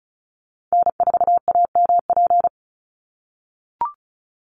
Morse code, heard as a keyed tone, about 36 wpm, 700 Hz. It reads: N4AMP
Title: USA Call Signs 35wpm